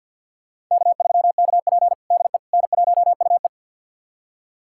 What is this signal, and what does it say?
Morse code, heard as a keyed tone, about 49 wpm, 700 Hz. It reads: K3KP DE N1UE
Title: USA Callsign DE Callsign - 4 Characters - 50wpm